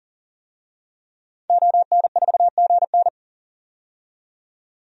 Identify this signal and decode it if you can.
Morse code, heard as a keyed tone, about 40 wpm, 700 Hz. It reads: ON4GN
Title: DXCC-100 Call Signs - 5 Characters - 40wpm